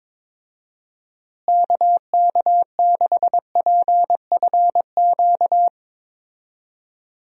Morse code, heard as a keyed tone, about 22 wpm, 700 Hz. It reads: KK6PFQ
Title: USA Call Signs - 6 Characters - Repeated 3x - 22wpm